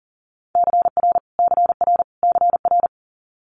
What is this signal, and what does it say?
Morse code, heard as a keyed tone, about 38 wpm, 700 Hz. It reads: CR CR CR
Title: Multi-letter Phonemes Repeated 3x 40wpm